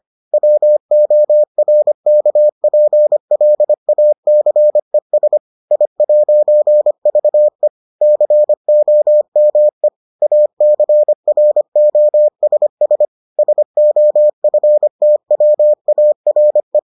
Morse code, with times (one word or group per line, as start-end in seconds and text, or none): 0.3-5.4 WORKPLACES
5.7-7.7 I'VE
8.0-9.9 COME
10.2-13.0 ACROSS
13.4-16.8 SOFTWARE